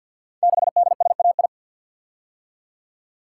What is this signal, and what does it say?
Morse code, heard as a keyed tone, about 48 wpm, 700 Hz. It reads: 6DSAI
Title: Code Groups - 5 Letter:Numbers - 50wpm